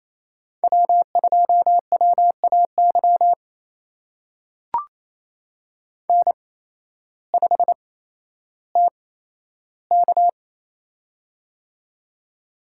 Morse code, heard as a keyed tone, about 28 wpm, spaced as for 13 wpm, 700 Hz. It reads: W2WAY N5TK
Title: USA Call Signs - 8x Character Spacing - 28wpm